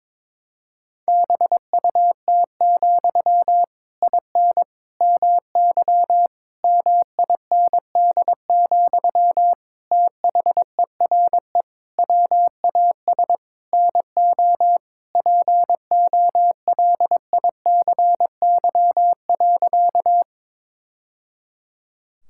Morse code, heard as a keyed tone, about 22 wpm, 700 Hz. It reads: BUT, IN MY MIND, THERE WAS NO POLICY.
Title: Sentences from Top 1000 Words - Review Part 5 - 22wpm